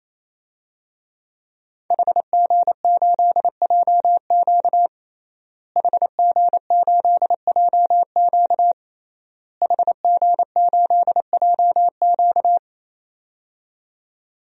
Morse code, heard as a keyed tone, about 28 wpm, 700 Hz. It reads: HG8JQ HG8JQ HG8JQ
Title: DXCC-100 Call Signs - 5 Characters - Repeated 3x - 28wpm